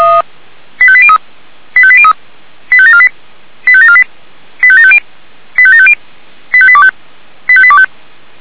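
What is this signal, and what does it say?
Five-tone selective call (ZVEI), heard as quick runs of five tones, tones of about 70 ms. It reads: 85803 85803 85748 85748 85750 85750 85825 85825
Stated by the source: ZVEI five-tone selective call received by SDR.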